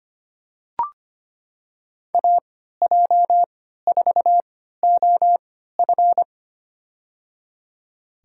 Morse code, heard as a keyed tone, about 25 wpm, spaced as for 14 wpm, 700 Hz. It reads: AJ4OF
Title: USA Call Signs - 5 Characters - 3x Character Spacing - 25wpm